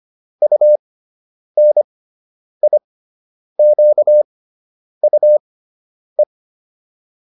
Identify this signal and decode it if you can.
Morse code, heard as a keyed tone, about 25 wpm, spaced as for 9 wpm, 600 Hz. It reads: UNIQUE